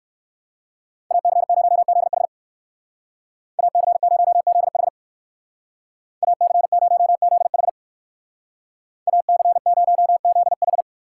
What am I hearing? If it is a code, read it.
Morse code, heard as a keyed tone, about 58 wpm, 700 Hz. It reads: AC0ZH AC0ZH AC0ZH AC0ZH